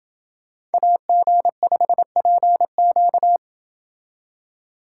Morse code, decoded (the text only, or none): AG5PQ